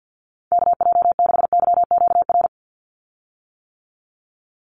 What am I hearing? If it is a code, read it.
Morse code, heard as a keyed tone, about 50 wpm, 700 Hz. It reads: KJ6YQR
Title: USA Call Signs 50wpm